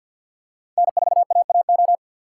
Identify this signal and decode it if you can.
Morse code, heard as a keyed tone, about 52 wpm, 700 Hz. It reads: N4AAO
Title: USA Call Signs - 2x Character Spacing - 50wpm